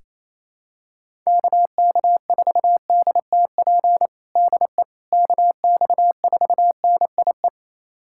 Morse code, heard as a keyed tone, about 28 wpm, 700 Hz. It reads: KK4DTP DE KX4NIE
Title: USA Callsign DE Callsign - 6 Characters - 28wpm